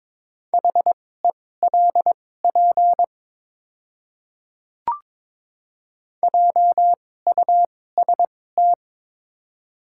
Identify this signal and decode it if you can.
Morse code, heard as a keyed tone, about 22 wpm, spaced as for 16 wpm, 700 Hz. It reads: HELP JUST
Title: 2x Character Spacing - Top 200 Words - 22wpm